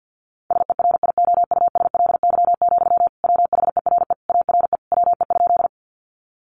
Morse code, heard as a keyed tone, about 50 wpm, 700 Hz. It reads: SERIOUSLY, WHERE ARE WE?
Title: Sentences from Top 1200 Words 50wpm